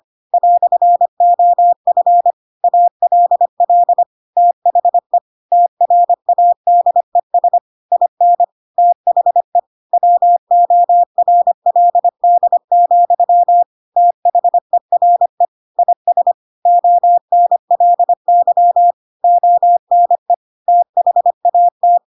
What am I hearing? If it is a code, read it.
Morse code, heard as a keyed tone, about 25 wpm, 700 Hz. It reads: "OF ALL THE TRADES IN THE WORLD, THERE IS ONLY ONE THAT
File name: Pinocchio / Pinocchio ch4 25 25 ik1hge